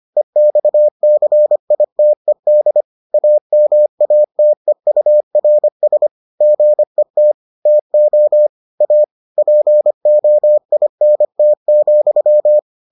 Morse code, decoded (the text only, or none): EXCITED AMATEURS GET TO A POINT,